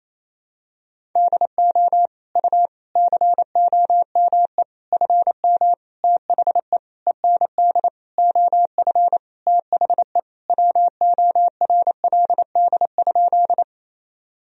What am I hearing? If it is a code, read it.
Morse code, heard as a keyed tone, about 28 wpm, 700 Hz. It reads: DO U COME FM THE END OF THE WORLD?
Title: Sentences from Top 200 Words and Abbreviations 28wpm